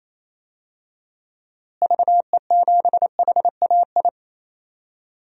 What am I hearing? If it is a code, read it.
Morse code, heard as a keyed tone, about 28 wpm, 700 Hz. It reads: VE7HAI
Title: DXCC-100 Call Signs - 6 Characters - Repeated 2x - 28wpm